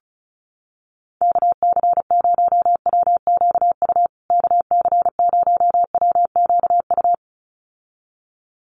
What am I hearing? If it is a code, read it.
Morse code, heard as a keyed tone, about 34 wpm, 700 Hz. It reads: KC0WQU KC0WQU
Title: USA Call Signs Repeated 2x 35wpm